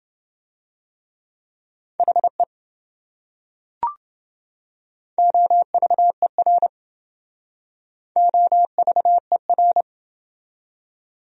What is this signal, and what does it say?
Morse code, heard as a keyed tone, about 28 wpm, 700 Hz. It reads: HE OVER OVER